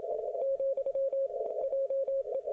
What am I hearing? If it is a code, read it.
Morse code, heard as a keyed tone, about 26 wpm, 540 Hz. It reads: , JE